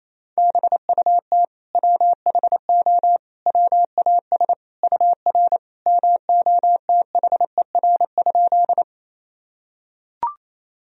Morse code, heard as a keyed tone, about 28 wpm, 700 Hz. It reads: BUT WHO WAS UR MOTHER?
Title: Sentences from Top 200 Words and Abbreviations 28wpm